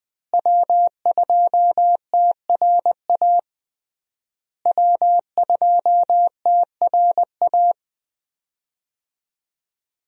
Morse code, heard as a keyed tone, about 20 wpm, 700 Hz. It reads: W2TRA W2TRA
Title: USA Call Signs - 5 Characters - Repeated 3x - 20wpm